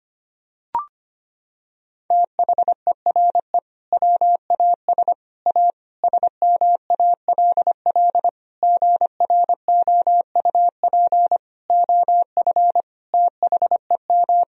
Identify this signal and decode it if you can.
Morse code, heard as a keyed tone, about 25 wpm, 700 Hz. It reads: THERE WAS A SMALL GROUP OF THEM
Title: Sentences from Top 1000 Words - Review Part 1 - 25wpm